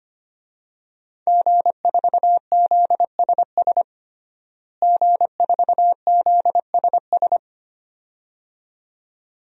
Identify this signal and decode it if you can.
Morse code, heard as a keyed tone, about 25 wpm, 700 Hz. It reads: G4ZSS G4ZSS
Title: DXCC-100 Call Signs - 5 Characters - Repeated 2x - 25wpm